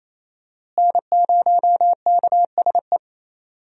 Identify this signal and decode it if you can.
Morse code, heard as a keyed tone, about 28 wpm, 700 Hz. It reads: N0KSE
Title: USA Call Signs - 5 Characters - Repeated 2x - 28wpm